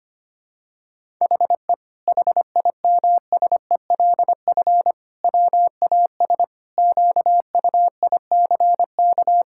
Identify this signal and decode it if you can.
Morse code, heard as a keyed tone, about 25 wpm, 700 Hz. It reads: HE HIMSELF WAS QUICK
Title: Sentences from Top 1600 Words 25wpm